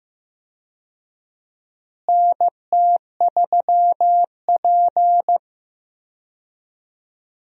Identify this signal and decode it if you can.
Morse code, heard as a keyed tone, about 15 wpm, 700 Hz. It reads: NT3P
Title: USA Call Signs - 4 Characters - Repeated 3x - 15wpm